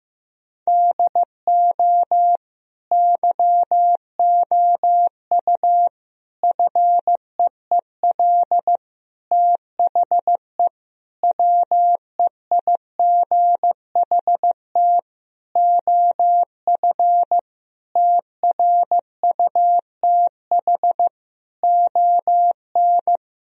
Morse code, heard as a keyed tone, about 15 wpm, 700 Hz. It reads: DO YOU FEEL THE WEIGHT OF TRUTH ON
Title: Sentences from Top 1000 Words - Review Part 3 - 15wpm